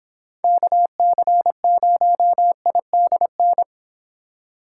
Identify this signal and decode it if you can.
Morse code, heard as a keyed tone, about 26 wpm, 700 Hz. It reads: KC0IDN